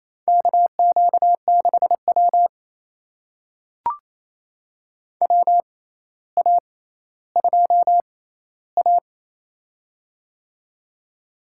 Morse code, heard as a keyed tone, about 28 wpm, spaced as for 14 wpm, 700 Hz. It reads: KQ6W WA2A
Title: USA Call Signs - 4 Characters - 6x Character Spacing - 28wpm